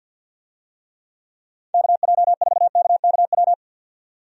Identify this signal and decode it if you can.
Morse code, heard as a keyed tone, about 47 wpm, 700 Hz. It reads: KJ4KKW
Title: USA Call Signs - 6 Characters - 50wpm